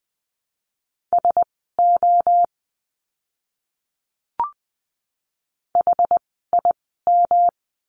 Morse code, heard as a keyed tone, about 20 wpm, spaced as for 14 wpm, 700 Hz. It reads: SO HIM
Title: 244 Lesson 39f - Letter z - Words 20wpm